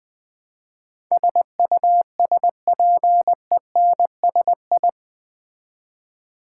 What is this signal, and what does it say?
Morse code, heard as a keyed tone, about 20 wpm, 700 Hz. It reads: SUSPENSI
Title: Build A Word - 4 Letters At A Time - 10-Letter Words - 20wpm